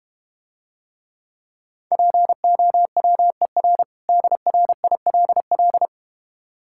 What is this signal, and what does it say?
Morse code, heard as a keyed tone, about 32 wpm, 700 Hz. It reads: POWER DRILL